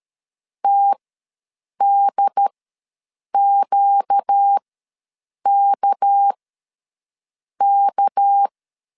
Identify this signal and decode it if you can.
Morse code, heard as a keyed tone, about 13 wpm, spaced as for 7 wpm, 800 Hz. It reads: TDQKK
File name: LEZIONI / 6 PKX0 / 6 PK ws 12wpm 3116